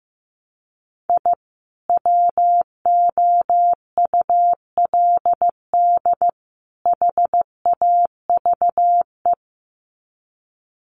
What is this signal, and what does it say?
Morse code, heard as a keyed tone, about 15 wpm, 700 Hz. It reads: I WOULD HAVE
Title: Sets of 3 Words 15wpm